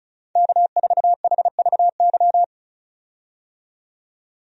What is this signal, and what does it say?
Morse code, heard as a keyed tone, about 34 wpm, 700 Hz. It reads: K4HVY
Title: USA Call Signs - 5 Characters - 35wpm